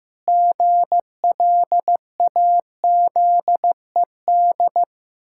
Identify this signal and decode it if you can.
Morse code, heard as a keyed tone, about 15 wpm, 700 Hz. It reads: GLAZED
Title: Pangram Challenge v2 - 15wpm